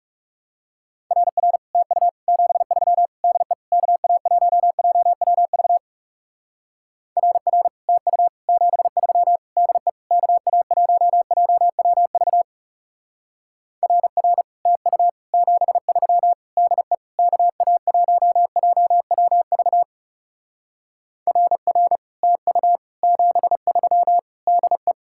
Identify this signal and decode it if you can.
Morse code, heard as a keyed tone, about 36 wpm, 700 Hz. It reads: RR TU 73 DE KA1JWV RR TU 73 DE KA1JWV RR TU 73 DE KA1JWV RR TU 73 DE
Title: Speed-Racing Common QSO Phrases Encore - 30wpm